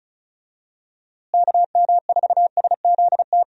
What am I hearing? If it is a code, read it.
Morse code, heard as a keyed tone, about 34 wpm, 700 Hz. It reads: KM4SZT
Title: USA Call Signs - 6 Characters - Repeated 3x - 35wpm